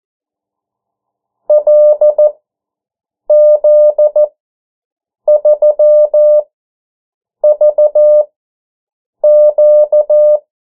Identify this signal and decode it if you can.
Morse code, heard as a keyed tone, about 14 wpm, spaced as for 7 wpm, 600 Hz. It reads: LZ3VQ
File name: morse mp3 / 09 urok 8